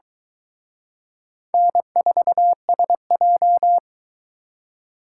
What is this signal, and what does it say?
Morse code, heard as a keyed tone, about 23 wpm, 700 Hz. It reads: N4SJ